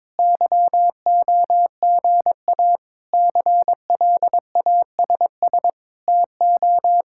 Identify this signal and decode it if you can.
Morse code, heard as a keyed tone, about 21 wpm, 700 Hz. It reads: YOGA CLASS TO
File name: Difficult Sentences with Hints - 22wpm